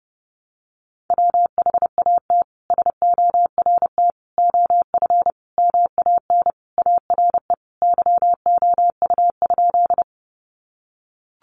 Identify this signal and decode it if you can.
Morse code, heard as a keyed tone, about 30 wpm, 700 Hz. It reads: WHAT SORT OF MAN ARE YOU?
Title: Sentences from Top 1300 Words 30wpm